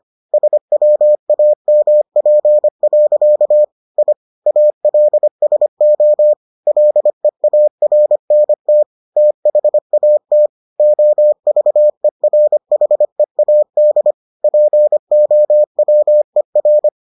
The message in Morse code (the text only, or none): SWAMP. I ALSO LEARNT THAT OVERHEAD POWER